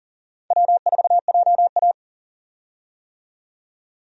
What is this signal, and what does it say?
Morse code, heard as a keyed tone, about 39 wpm, 700 Hz. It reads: W4JA
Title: USA Call Signs - 4 Characters - 40wpm